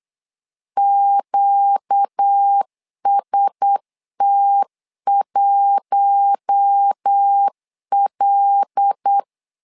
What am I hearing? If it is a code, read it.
Morse code, heard as a keyed tone, about 8 wpm, 800 Hz. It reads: QST1L
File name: LEZIONI / 5 LBVC / 5 LBVC 8wpm 2935